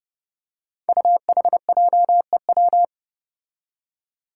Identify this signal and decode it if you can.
Morse code, heard as a keyed tone, about 30 wpm, 700 Hz. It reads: UHJEW
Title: Five Challenging Letter-Number Combinations 30wpm